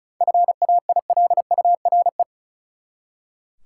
Morse code, heard as a keyed tone, about 34 wpm, 700 Hz. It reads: FAILURE
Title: Seven Letter Words - 35wpm